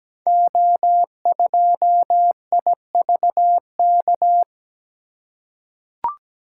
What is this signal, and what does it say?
Morse code, heard as a keyed tone, about 17 wpm, 700 Hz. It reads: O2IVK
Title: Code Groups - 5 Letter:Numbers - 17wpm